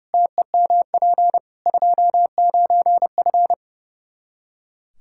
Morse code, heard as a keyed tone, about 28 wpm, 700 Hz. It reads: TEMP 29F
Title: Practice QSO - Exchange 2 - Info - 30wpm